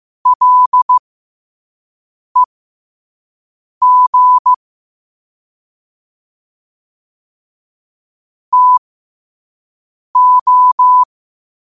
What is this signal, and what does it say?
Morse code, heard as a keyed tone, about 15 wpm, spaced as for 5 wpm, 1000 Hz. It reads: LEG TO
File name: qotdfiles / 5wpm